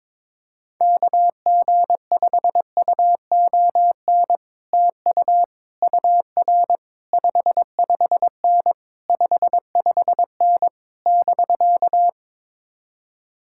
Morse code, heard as a keyed tone, about 22 wpm, 700 Hz. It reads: KG5UON TU UR 55N 55N <BK>
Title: Speed-Racing Common QSO Phrases - 22wpm